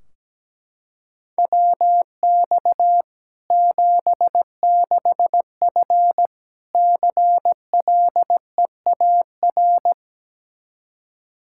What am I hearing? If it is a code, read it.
Morse code, heard as a keyed tone, about 17 wpm, 700 Hz. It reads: WX 76F CLEAR
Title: K1USN SST QSO Practice 17wpm